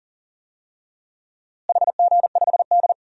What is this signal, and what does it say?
Morse code, heard as a keyed tone, about 38 wpm, 700 Hz. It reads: HG5D
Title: DXCC-100 Call Signs - 4 Characters - Repeated 2x - 40wpm